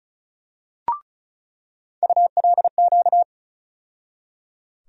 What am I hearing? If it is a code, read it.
Morse code, heard as a keyed tone, about 36 wpm, 700 Hz. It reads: ULQ
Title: USA Call Sign Postfixes - 3 Letters 35wpm